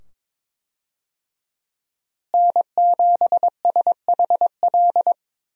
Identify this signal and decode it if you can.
Morse code, heard as a keyed tone, about 21 wpm, 700 Hz. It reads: N7SHL
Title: USA Call Signs - 5 Characters - Repeated 2x - 22wpm